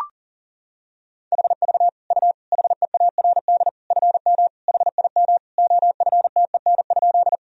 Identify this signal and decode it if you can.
Morse code, heard as a keyed tone, about 40 wpm, 700 Hz. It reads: HV U HEARD FM HIM OFTEN?
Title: Sentences from Top 400 Words and Abbreviations 40wpm